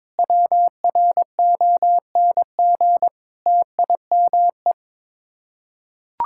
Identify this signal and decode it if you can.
Morse code, heard as a keyed tone, about 22 wpm, 700 Hz. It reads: WRONG TIME
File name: practice / 22wpm / Sets of 7 Words - 22wpm